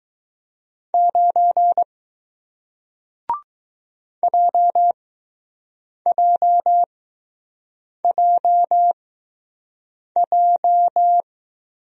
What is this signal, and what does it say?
Morse code, heard as a keyed tone, about 22 wpm, spaced as for 6 wpm, 700 Hz. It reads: 9 JJJJ